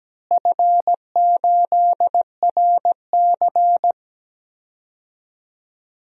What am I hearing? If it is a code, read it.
Morse code, heard as a keyed tone, about 17 wpm, 700 Hz. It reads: F8RC
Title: DXCC-100 Call Signs - 4 Characters - Repeated 2x - 17wpm